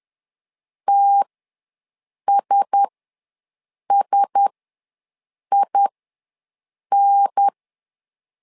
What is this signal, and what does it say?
Morse code, heard as a keyed tone, about 10 wpm, spaced as for 6 wpm, 800 Hz. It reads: TSSIN
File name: LEZIONI / 4 Q1S ws 10wpm 2208